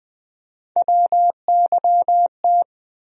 Morse code, heard as a keyed tone, about 20 wpm, 700 Hz. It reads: WYT
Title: USA Call Sign Postfixes - 3 Letters 20wpm